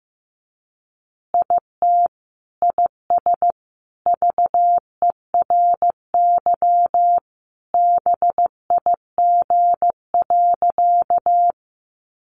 Morse code, heard as a keyed tone, about 15 wpm, 700 Hz. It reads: IT IS VERY BIG.
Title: Easy Phrases 15wpm